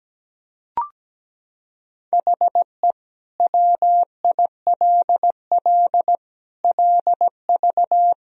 Morse code, heard as a keyed tone, about 17 wpm, 700 Hz. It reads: HE WILL LV
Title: Sentences from Top 400 Words and Abbreviations 17wpm